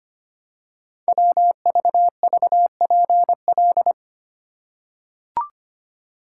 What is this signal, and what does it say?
Morse code, heard as a keyed tone, about 25 wpm, 700 Hz. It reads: WVVPL
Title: Code Groups - 5 Letter:Numbers - 25wpm